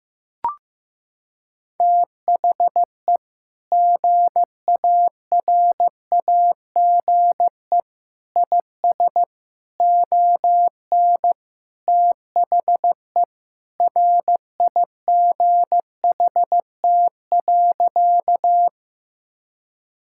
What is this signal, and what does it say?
Morse code, heard as a keyed tone, about 15 wpm, 700 Hz. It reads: THE GARAGE IS ON THE RIGHT.
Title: Sentences from Top 1600 Words 15wpm